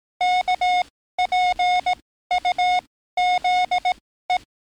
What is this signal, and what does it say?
Morse code, heard as a keyed tone, about 18 wpm, spaced as for 13 wpm, 730 Hz. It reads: KPUZE
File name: training / NAVPERS 11049to11060 / NAVPERS 11056 RA RB